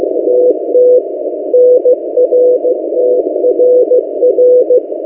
Morse code, heard as a keyed tone, about 16 wpm, 500 Hz. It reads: TT NRTRR
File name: M08a- -1355utc-8096khz